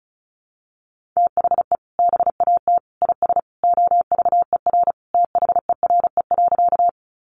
Morse code, heard as a keyed tone, about 34 wpm, 700 Hz. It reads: THE BAT IS OVER THERE.
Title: Sentences from Top 1500 Words 35wpm